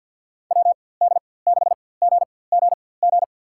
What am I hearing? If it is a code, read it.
Morse code, heard as a keyed tone, about 52 wpm, spaced as for 24 wpm, 690 Hz. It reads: WD6GGG
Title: USA Call Signs - 4x Character Spacing - 50wpm